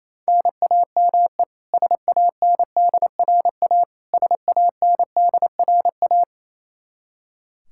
Morse code, heard as a keyed tone, about 27 wpm, 700 Hz. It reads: NAME SANDRA SANDRA
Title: Practice QSO - Exchange 1 - Beginning - 28wpm